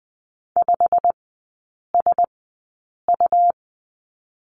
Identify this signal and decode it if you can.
Morse code, heard as a keyed tone, about 20 wpm, spaced as for 8 wpm, 700 Hz. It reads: 5SU
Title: 100 Lesson 17e - Letter u - Three Characters 20wpm